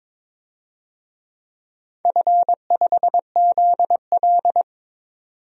Morse code, heard as a keyed tone, about 22 wpm, 690 Hz. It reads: F5ZL
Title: DXCC-100 Call Signs - 4 Characters - 22wpm